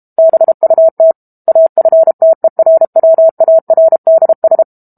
Morse code, heard as a keyed tone, about 32 wpm, 650 Hz. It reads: BUT AFTERWARDS
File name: Book Two / 06-Chapter 6 - 32 WPM